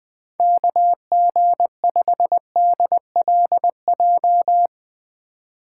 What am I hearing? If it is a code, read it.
Morse code, heard as a keyed tone, about 20 wpm, 700 Hz. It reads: KG5DLJ
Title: USA Call Signs - 6 Characters - 4x Character Spacing - 20wpm